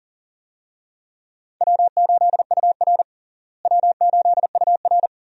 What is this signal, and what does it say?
Morse code, heard as a keyed tone, about 40 wpm, 700 Hz. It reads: W8UR W8UR
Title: USA Call Signs - 4 Characters - Repeated 2x - 40wpm